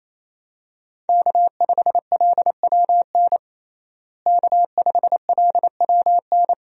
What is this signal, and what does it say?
Morse code, heard as a keyed tone, about 28 wpm, 700 Hz. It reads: K5LWN K5LWN
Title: USA Call Signs - 5 Characters - Repeated 3x - 28wpm